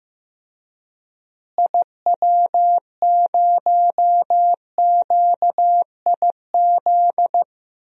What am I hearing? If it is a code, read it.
Morse code, heard as a keyed tone, about 15 wpm, 700 Hz. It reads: IW0QIZ
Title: DXCC-100 Call Signs - 6 Characters - Repeated 3x - 15wpm